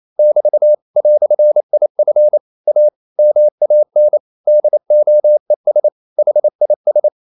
Morse code, heard as a keyed tone, about 28 wpm, 600 Hz. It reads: <BT> "IF A MAN DOES HIS
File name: CW / Practice / EnglishQuotations0128